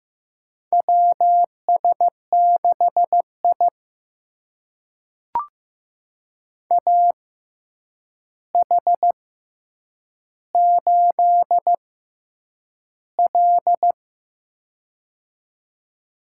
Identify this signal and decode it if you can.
Morse code, heard as a keyed tone, about 15 wpm, spaced as for 8 wpm, 700 Hz. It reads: WS6I AH8L
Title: USA Call Signs - 4 Characters - 6x Character Spacing - 15wpm